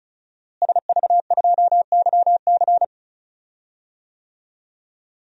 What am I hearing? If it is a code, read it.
Morse code, heard as a keyed tone, about 36 wpm, 700 Hz. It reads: SV2YC
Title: DXCC-100 Call Signs - 5 Characters - Repeated 3x - 35wpm